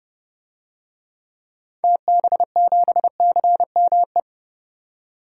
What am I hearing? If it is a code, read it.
Morse code, heard as a keyed tone, about 30 wpm, 700 Hz. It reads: TB7CME
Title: DXCC-100 Call Signs - 6 Characters - 30wpm